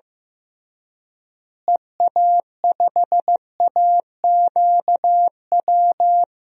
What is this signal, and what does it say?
Morse code, heard as a keyed tone, about 15 wpm, 700 Hz. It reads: EA5AQW
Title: DXCC-100 Call Signs - 6 Characters - Repeated 3x - 15wpm